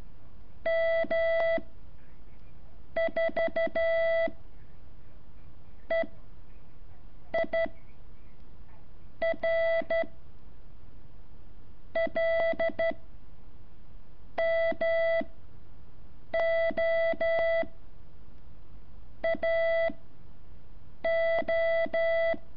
Morse code, heard as a keyed tone, about 12 wpm, 660 Hz. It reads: M 4 E I R L M O A O
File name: curso / ejercicio 8